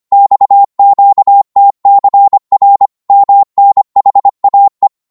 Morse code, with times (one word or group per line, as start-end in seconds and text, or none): 0.1-2.9 XQTCR
3.1-4.9 MNHAE